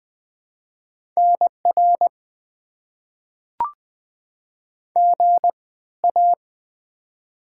Morse code, heard as a keyed tone, about 20 wpm, spaced as for 14 wpm, 690 Hz. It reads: NR GA
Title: Top 90 QSO Elements - 3x Character Spacing - 20wpm